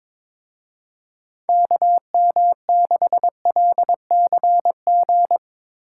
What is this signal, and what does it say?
Morse code, heard as a keyed tone, about 22 wpm, 700 Hz. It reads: KM6LCG
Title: Speed Racing - USA 2x3 Call Signs 22wpm